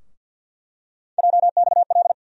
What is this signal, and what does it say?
Morse code, heard as a keyed tone, about 47 wpm, 700 Hz. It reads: JXL
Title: USA Call Sign Postfixes - 3 Letters 50wpm